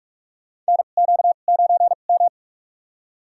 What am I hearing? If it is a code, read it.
Morse code, heard as a keyed tone, about 46 wpm, spaced as for 32 wpm, 700 Hz. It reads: NQ9M
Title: USA Call Signs - 4 Characters - 2x Character Spacing - 45wpm